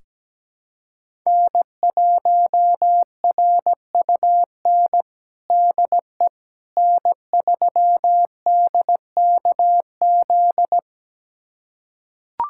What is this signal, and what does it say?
Morse code, heard as a keyed tone, about 17 wpm, 700 Hz. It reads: N1RUN DE N3DKZ
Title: USA Callsign DE Callsign - 5 Characters - 17wpm